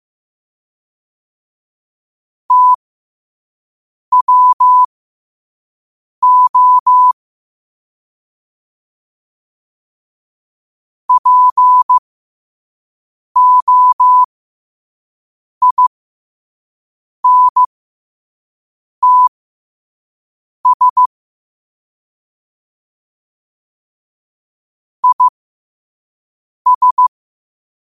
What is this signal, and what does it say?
Morse code, heard as a keyed tone, about 15 wpm, spaced as for 5 wpm, 1000 Hz. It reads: TWO POINTS IS
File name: qotdfiles / 5wpm